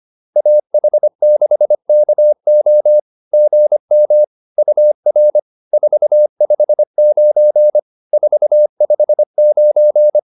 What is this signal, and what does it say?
Morse code, heard as a keyed tone, about 25 wpm, 600 Hz. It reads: AH6KO GM UR 459 459